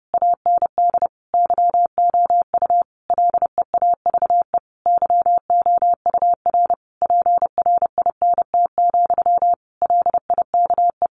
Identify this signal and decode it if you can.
Morse code, heard as a keyed tone, about 30 wpm, 700 Hz. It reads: AND YOU LEAVE YOUR PRINT, LIKE
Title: A Sound Of Thunder 30wpm